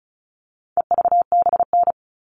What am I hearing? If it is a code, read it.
Morse code, heard as a keyed tone, about 34 wpm, 700 Hz. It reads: EVBN
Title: All Cut Numbers - 4-Digit 35wpm